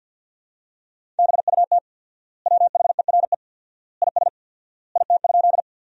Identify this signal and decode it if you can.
Morse code, heard as a keyed tone, about 49 wpm, 700 Hz. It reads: BUT WHERE IS IT?
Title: 4x Word-Spacing - Sentences from Top 200 Words - 50wpm